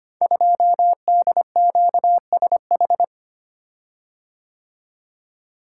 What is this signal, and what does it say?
Morse code, heard as a keyed tone, about 25 wpm, 700 Hz. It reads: 2DQSH
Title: Code Groups - 5 Letter:Numbers - 25wpm